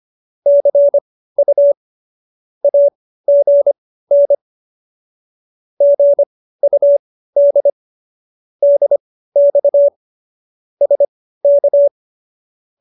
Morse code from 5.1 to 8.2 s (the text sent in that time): GUD